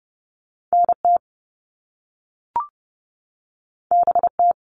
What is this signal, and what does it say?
Morse code, heard as a keyed tone, about 30 wpm, 700 Hz. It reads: N T B T
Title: All Cut Numbers - 2-Digit 30wpm